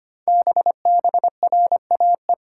Morse code, heard as a keyed tone, about 25 wpm, 700 Hz. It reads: BBRAE
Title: Code Groups - 5 Letters - 25wpm